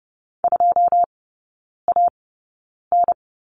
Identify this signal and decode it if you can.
Morse code, heard as a keyed tone, about 30 wpm, spaced as for 9 wpm, 700 Hz. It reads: 2AN
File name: 081 Lesson 14e - Number 2 - Three Characters 30wpm